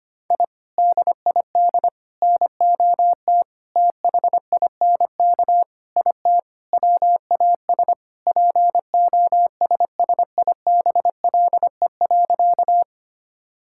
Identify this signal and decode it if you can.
Morse code, heard as a keyed tone, about 25 wpm, 700 Hz. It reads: I DID NOT THINK IT WAS POSSIBLE.
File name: practice / 25wpm / Sentences with 4-Word Phrases - 25wpm